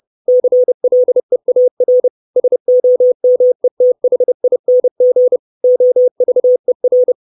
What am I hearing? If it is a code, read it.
Morse code, heard as a keyed tone, about 30 wpm, 500 Hz. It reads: CLEAR SOMETHING OVER
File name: 632 Words 30WPM